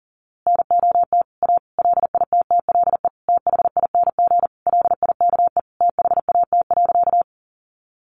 Morse code, heard as a keyed tone, about 40 wpm, 700 Hz. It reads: NOT A LITTLE THING LIKE THAT.
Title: A Sound Of Thunder 40wpm